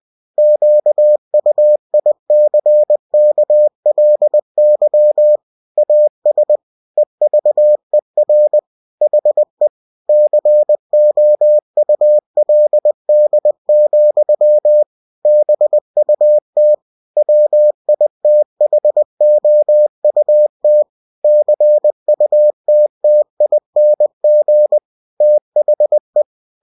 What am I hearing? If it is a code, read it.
Morse code, heard as a keyed tone, about 20 wpm, 600 Hz. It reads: QUICKLY AS EVER HE COULD, BUT WITHOUT CUTTING THE
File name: cw-books / winnie-the-pooh / winnie-20wpm-page-0109